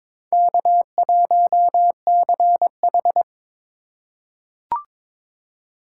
Morse code, heard as a keyed tone, about 22 wpm, 700 Hz. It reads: K1CH
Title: USA Call Signs - 4x Character Spacing - 22wpm